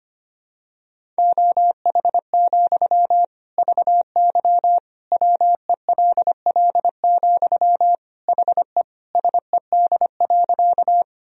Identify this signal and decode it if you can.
Morse code, heard as a keyed tone, about 25 wpm, 700 Hz. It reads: OH, VY WELL, HE SED.
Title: Sentences from Top 500 Words and Abbreviations 25wpm